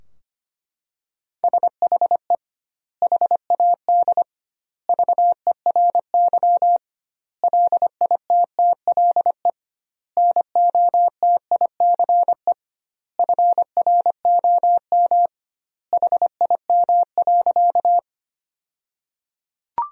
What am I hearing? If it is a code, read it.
Morse code, heard as a keyed tone, about 25 wpm, 700 Hz. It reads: SHE HAD VERY LITTLE NOTICE FROM HIM.
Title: 2x Word-Spacing - Sentences from Top 400 Words - 25wpm